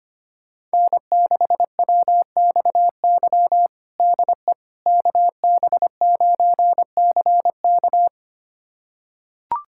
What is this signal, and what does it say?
Morse code, heard as a keyed tone, about 25 wpm, 700 Hz. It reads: N6WXY DE KB9CK
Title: USA Callsign DE Callsign - 5 Characters - 25wpm